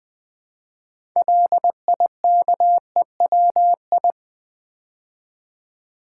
Morse code, heard as a keyed tone, about 20 wpm, 700 Hz. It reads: LIKEWI
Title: Build A Word - 3 Letters At A Time - 8-Letter Words - 20wpm